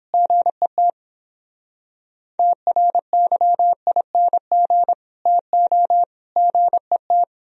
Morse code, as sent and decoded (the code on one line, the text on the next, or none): --. - / - .-. -.-- .. -. --. / - --- / --. -
GET TRYING TO GET